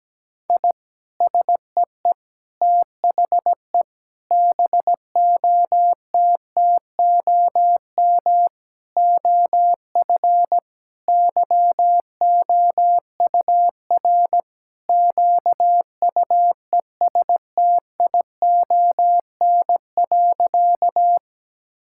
Morse code, heard as a keyed tone, about 17 wpm, 700 Hz. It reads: I SEE THE BOTTOM OF YOUR QUESTION.
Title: Sentences from Top 1000 Words - Review Part 6 - 17wpm